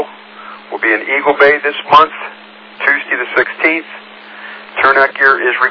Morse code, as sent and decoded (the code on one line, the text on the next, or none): none